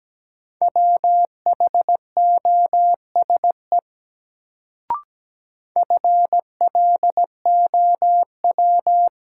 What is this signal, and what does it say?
Morse code, heard as a keyed tone, about 17 wpm, 700 Hz. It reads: WHOSE FLOW
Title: Sentences from Top 1200 Words - New Word Drill - 17wpm